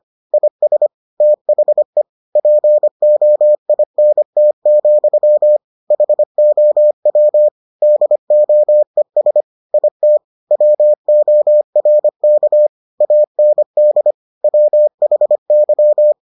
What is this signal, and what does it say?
Morse code, heard as a keyed tone, about 25 wpm, 600 Hz. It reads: IS THE POINT, HOW DOES IT WORK AND WHY